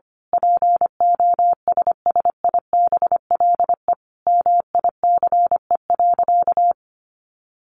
Morse code, heard as a keyed tone, about 25 wpm, 700 Hz. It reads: POSSIBLE MICE.
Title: A Sound Of Thunder 25wpm